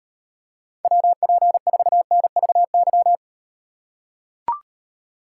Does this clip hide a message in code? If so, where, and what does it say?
Yes: Morse code from 0.8 to 3.2 s, heard as a keyed tone, about 38 wpm, 700 Hz. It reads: WP4NVY